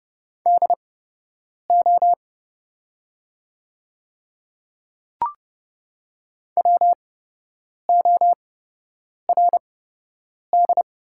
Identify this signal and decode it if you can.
Morse code, heard as a keyed tone, about 29 wpm, spaced as for 8 wpm, 700 Hz. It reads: DO WORD
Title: 8x Character Spacing - Top 100 Words - 30wpm